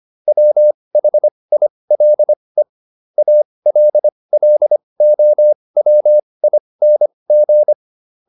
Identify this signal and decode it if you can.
Morse code, heard as a keyed tone, about 25 wpm, spaced as for 20 wpm, 600 Hz. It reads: WHILE ALLOWING